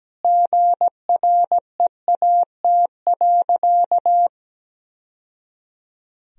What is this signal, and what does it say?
Morse code, heard as a keyed tone, about 17 wpm, 700 Hz. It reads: GREAT.
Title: Sentences from Top 1000 Words - Review Part 1 - 17wpm